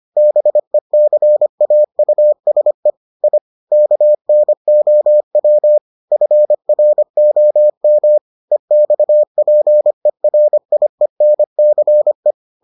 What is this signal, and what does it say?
Morse code, heard as a keyed tone, about 25 wpm, 600 Hz. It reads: BECAUSE I KNOW FROM EXPERIENCE